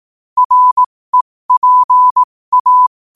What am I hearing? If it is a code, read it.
Morse code, heard as a keyed tone, about 18 wpm, spaced as for 15 wpm, 1000 Hz. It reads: REPA
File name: qotdfiles / 15wpm